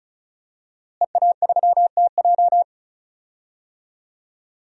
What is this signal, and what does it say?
Morse code, heard as a keyed tone, about 34 wpm, 700 Hz. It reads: EA3TJ
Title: DXCC-100 Call Signs - 5 Characters - Repeated 3x - 35wpm